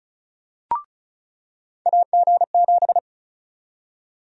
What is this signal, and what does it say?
Morse code, heard as a keyed tone, about 36 wpm, 700 Hz. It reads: AG7
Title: USA Call Sign Prefixes - 35wpm